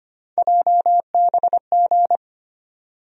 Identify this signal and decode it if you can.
Morse code, heard as a keyed tone, about 25 wpm, 700 Hz. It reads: JBG
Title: USA Call Sign Postfixes - 3 Letters 25wpm